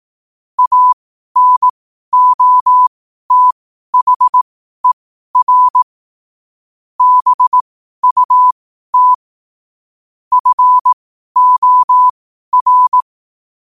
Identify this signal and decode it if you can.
Morse code, heard as a keyed tone, about 18 wpm, spaced as for 12 wpm, 1000 Hz. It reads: ANOTHER BUT FOR
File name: qotdfiles / 12wpm